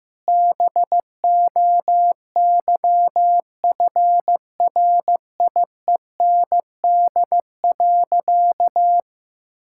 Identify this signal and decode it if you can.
Morse code, heard as a keyed tone, about 15 wpm, 700 Hz. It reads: BOYFRIEND.
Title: Sentences from Top 1000 Words - Review Part 5 - 15wpm